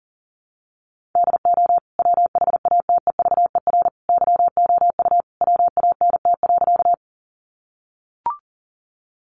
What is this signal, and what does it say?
Morse code, heard as a keyed tone, about 40 wpm, 700 Hz. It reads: DO WHATEVER YOU WANT.
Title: Sentences from Top 1100 Words 40wpm